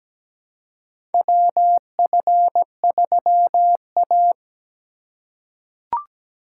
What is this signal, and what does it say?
Morse code, heard as a keyed tone, about 17 wpm, 700 Hz. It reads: WF3A
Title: USA Call Signs - 4x Character Spacing - 17wpm